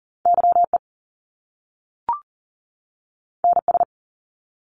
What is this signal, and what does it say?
Morse code, heard as a keyed tone, about 38 wpm, 700 Hz. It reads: YE NS
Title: Top Two Letter Combinations in Words 40wpm